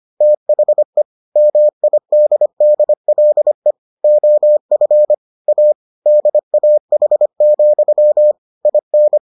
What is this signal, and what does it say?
Morse code, heard as a keyed tone, about 25 wpm, 600 Hz. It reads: THE MIDDLE OF A DAH, IN